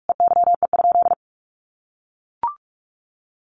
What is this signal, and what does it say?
Morse code, heard as a keyed tone, about 44 wpm, 700 Hz. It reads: EYE?
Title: Sentences from Top 500 Words 45wpm